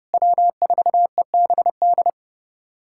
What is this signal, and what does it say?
Morse code, heard as a keyed tone, about 32 wpm, 700 Hz. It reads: W4EBD
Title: USA Callsign DE Callsign 30wpm